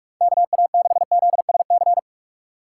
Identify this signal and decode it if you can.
Morse code, heard as a keyed tone, about 43 wpm, 700 Hz. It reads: KA6ZSC
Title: USA Call Signs - 6 Characters - 2x Character Spacing - 45wpm